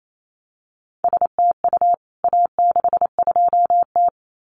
Morse code, heard as a keyed tone, about 28 wpm, 700 Hz. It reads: STU A62T
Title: CWT Exchange 28wpm